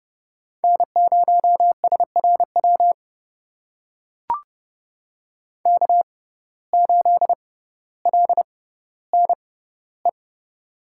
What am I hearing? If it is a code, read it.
Morse code, heard as a keyed tone, about 30 wpm, spaced as for 15 wpm, 700 Hz. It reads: N0SRW K8LNE
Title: USA Call Signs - 5 Characters - 6x Character Spacing - 30wpm